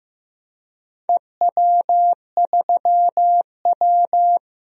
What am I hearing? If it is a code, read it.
Morse code, heard as a keyed tone, about 15 wpm, 700 Hz. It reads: EW3W
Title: DXCC-100 Call Signs - 4 Characters - Repeated 2x - 15wpm